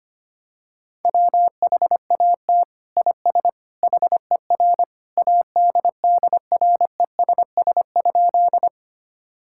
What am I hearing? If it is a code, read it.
Morse code, heard as a keyed tone, about 25 wpm, 700 Hz. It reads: WHAT IS HER ADDRESS?
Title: Sentences from Top 1000 Words - Review Part 6 - 25wpm